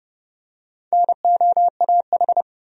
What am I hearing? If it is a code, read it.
Morse code, heard as a keyed tone, about 31 wpm, 700 Hz. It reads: NOAH
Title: USA Common Names - Max 4 Letters - 30wpm